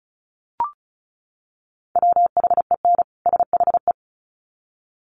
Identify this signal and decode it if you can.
Morse code, heard as a keyed tone, about 34 wpm, 700 Hz. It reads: WHEN SHE
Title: Sets of 2 Words 35wpm